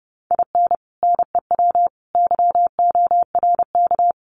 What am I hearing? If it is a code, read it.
Morse code, heard as a keyed tone, about 29 wpm, 700 Hz. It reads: IN NEW YORK
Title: Sets of 4 Words 30wpm